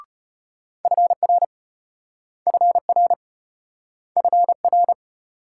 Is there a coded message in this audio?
Yes: Morse code from 0.8 to 4.9 s, heard as a keyed tone, about 34 wpm, 700 Hz. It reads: FR FR FR